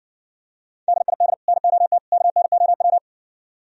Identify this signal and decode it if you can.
Morse code, heard as a keyed tone, about 62 wpm, 700 Hz. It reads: DID NOT KNOW